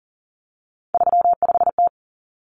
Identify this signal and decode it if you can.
Morse code, heard as a keyed tone, about 40 wpm, 700 Hz. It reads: 35T
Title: Common Cut Numbers - 3-Digit 40wpm